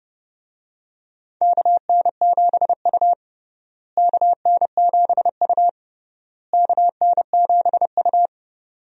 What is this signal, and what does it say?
Morse code, heard as a keyed tone, about 30 wpm, 700 Hz. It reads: KN7U KN7U KN7U
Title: USA Call Signs - 4 Characters - Repeated 3x - 30wpm